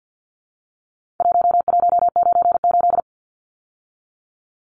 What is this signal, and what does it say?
Morse code, heard as a keyed tone, about 50 wpm, 700 Hz. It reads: 1198
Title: Numbers - Contest Style - 4-Digit 50wpm